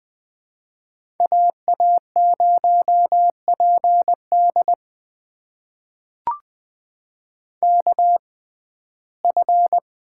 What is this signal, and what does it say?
Morse code, heard as a keyed tone, about 20 wpm, spaced as for 14 wpm, 700 Hz. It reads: AA0PD KF
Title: USA Call Signs - 5 Characters - 6x Character Spacing - 20wpm